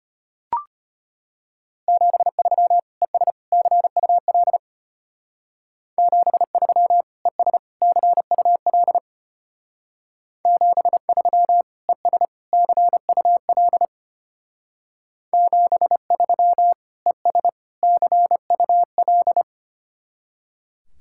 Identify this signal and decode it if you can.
Morse code, heard as a keyed tone, about 31 wpm, 700 Hz. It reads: 73 ES CUL 73 ES CUL 73 ES CUL 73 ES CUL
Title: Speed-Racing Common QSO Phrases - 25wpm